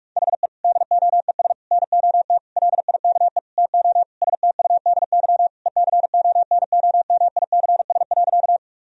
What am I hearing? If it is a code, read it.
Morse code, heard as a keyed tone, about 41 wpm, 700 Hz. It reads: HE DOES NOT LIKE TO STUDY ECONOMICS.
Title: Sentences from Top 1000 Words - Review Part 3 - 45wpm